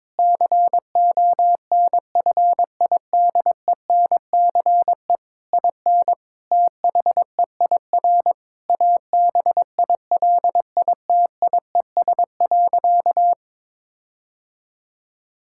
Morse code, heard as a keyed tone, about 22 wpm, 700 Hz. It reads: CONFIDENCE IN THEIR ABILITIES.
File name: Difficult Sentences with Hints - 22wpm